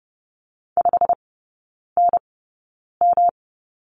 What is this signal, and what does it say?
Morse code, heard as a keyed tone, about 30 wpm, spaced as for 9 wpm, 700 Hz. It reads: 5NM
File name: Lesson 18e - Letter m - Three Characters 30wpm